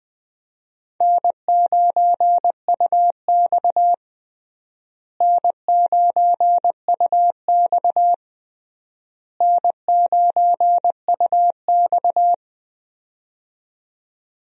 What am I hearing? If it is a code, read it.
Morse code, heard as a keyed tone, about 19 wpm, 700 Hz. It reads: N9UX N9UX N9UX
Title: USA Call Signs - 4 Characters - Repeated 3x - 20wpm